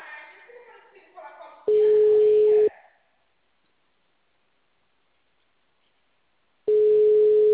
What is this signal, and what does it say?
Ringback tone (Europe, 425 Hz), with speech.